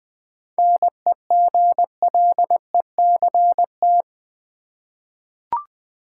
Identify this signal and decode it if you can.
Morse code, heard as a keyed tone, about 20 wpm, 700 Hz. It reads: NEGLECT
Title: 7 Letter Words - First 3 Letters Given - 20wpm